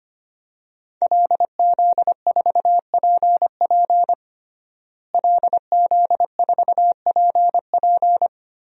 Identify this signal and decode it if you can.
Morse code, heard as a keyed tone, about 25 wpm, 700 Hz. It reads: LZ4PP LZ4PP
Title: DXCC-100 Call Signs - 5 Characters - Repeated 3x - 25wpm